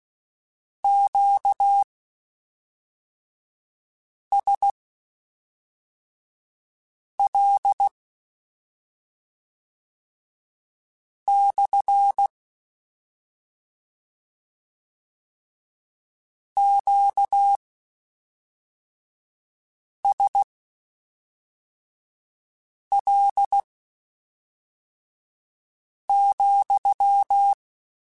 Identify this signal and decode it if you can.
Morse code, heard as a keyed tone, about 16 wpm, spaced as for 3 wpm, 800 Hz. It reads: QSL / QSL,
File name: Q Calls / Q Calls-mp3s / Q Calls SPSL2 @5